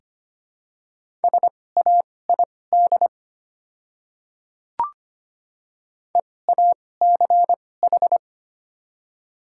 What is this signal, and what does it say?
Morse code, heard as a keyed tone, about 25 wpm, spaced as for 18 wpm, 700 Hz. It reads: SAID EACH
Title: 2x Character Spacing - Top 200 Words - 25wpm